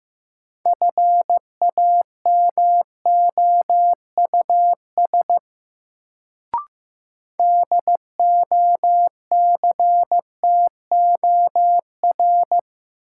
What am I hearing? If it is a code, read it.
Morse code, heard as a keyed tone, about 15 wpm, 700 Hz. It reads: FAMOUS DOCTOR
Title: Sentences from Top 1500 Words - New Word Drill - 15wpm